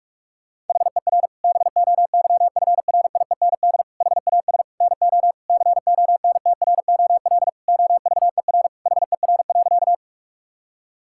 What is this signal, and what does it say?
Morse code, heard as a keyed tone, about 46 wpm, 700 Hz. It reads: HER BOYFRIEND HAS NO CONTROL OVER HER.
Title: Sentences from Top 1000 Words - Review Part 5 - 45wpm